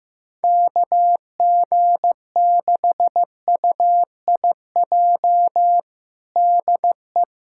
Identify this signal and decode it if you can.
Morse code, heard as a keyed tone, about 15 wpm, 700 Hz. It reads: KG6UIJ DE
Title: USA Callsign DE Callsign - 6 Characters - 15wpm